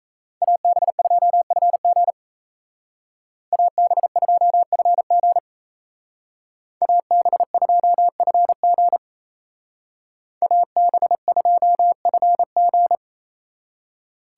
Morse code, heard as a keyed tone, about 33 wpm, 700 Hz. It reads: AB2FG AB2FG AB2FG AB2FG
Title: Speed Racing - USA 2x2 Call Signs 28wpm